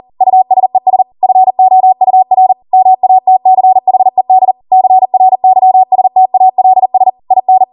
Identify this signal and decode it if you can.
Morse code, heard as a keyed tone, about 40 wpm, 750 Hz. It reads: USES FOUR MATCHED CRYSTALS IN
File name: Morse / 40WPM